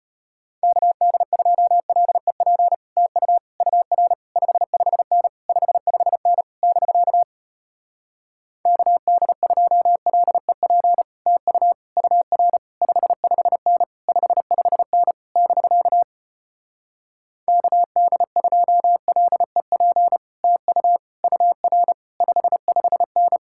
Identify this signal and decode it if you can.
Morse code, heard as a keyed tone, about 33 wpm, 700 Hz. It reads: KD2LEP TU UR 55N 55N <BK> KD2LEP TU UR 55N 55N <BK> KD2LEP TU UR 55N